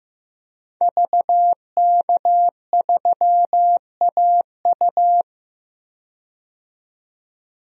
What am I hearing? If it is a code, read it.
Morse code, heard as a keyed tone, about 15 wpm, 700 Hz. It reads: VK3AU
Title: DXCC-100 Call Signs - 5 Characters - 15wpm